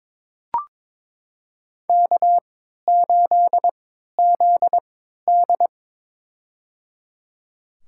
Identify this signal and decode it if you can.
Morse code, heard as a keyed tone, about 22 wpm, spaced as for 13 wpm, 700 Hz. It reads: K8ZD
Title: USA Call Signs - 3x Character Spacing - 22wpm